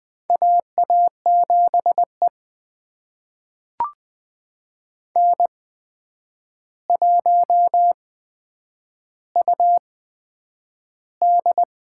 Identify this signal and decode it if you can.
Morse code, heard as a keyed tone, about 20 wpm, spaced as for 9 wpm, 700 Hz. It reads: AA7E N1UD
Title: USA Call Signs - 4 Characters - 8x Character Spacing - 20wpm